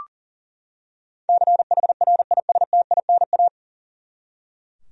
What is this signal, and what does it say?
Morse code, heard as a keyed tone, about 40 wpm, 700 Hz. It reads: CHRISTINA
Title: USA Common Names - 40wpm